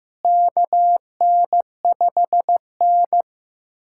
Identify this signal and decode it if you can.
Morse code, heard as a keyed tone, about 15 wpm, 700 Hz. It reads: KN5N
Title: USA Callsign DE Callsign - 4 Characters - 15wpm